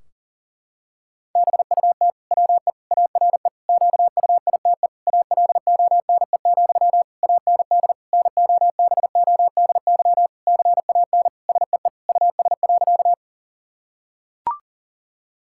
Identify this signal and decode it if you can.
Morse code, heard as a keyed tone, about 40 wpm, 700 Hz. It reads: BUT WE ARE QUITE ALONE, AND NOBODY CAN SEE US.
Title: Sentences from Top 1600 Words 40wpm